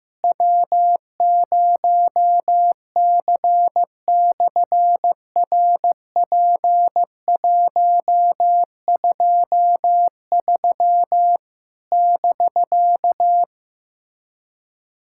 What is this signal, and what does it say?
Morse code, heard as a keyed tone, about 15 wpm, 700 Hz. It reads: W0C/RP123 <BK>
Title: SOTA QSO Practice 15wpm